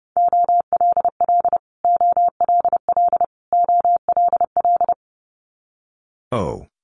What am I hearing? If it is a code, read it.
Morse code, heard as a keyed tone, about 30 wpm, 700 Hz. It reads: OLL OLL OLL
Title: Multi-letter Phonemes Repeated 3x 30wpm